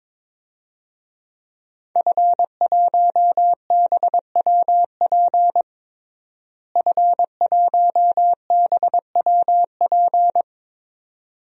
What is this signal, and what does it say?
Morse code, heard as a keyed tone, about 22 wpm, 700 Hz. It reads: F1BWP F1BWP
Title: DXCC-100 Call Signs - 5 Characters - Repeated 2x - 22wpm